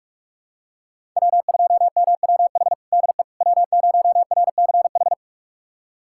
Morse code, heard as a keyed tone, about 42 wpm, 700 Hz. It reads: W2MWH DE W0RCH
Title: USA Callsign DE Callsign - 5 Characters - 45wpm